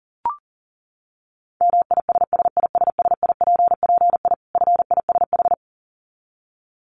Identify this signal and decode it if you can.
Morse code, heard as a keyed tone, about 40 wpm, 700 Hz. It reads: MISSISSIPPI FISH
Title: Dit Challenge 40wpm